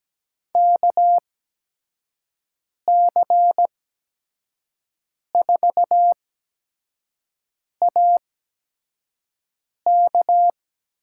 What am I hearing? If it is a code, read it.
Morse code, heard as a keyed tone, about 17 wpm, spaced as for 5 wpm, 700 Hz. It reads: KC4AK
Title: USA Call Signs - 6 Characters - 8x Character Spacing - 17wpm